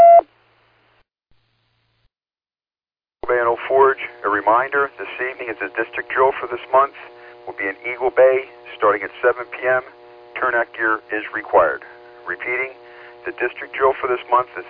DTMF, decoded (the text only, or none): none